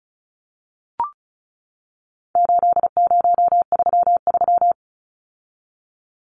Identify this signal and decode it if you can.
Morse code, heard as a keyed tone, about 34 wpm, 700 Hz. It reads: 8033
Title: Numbers - 4-Digit 35wpm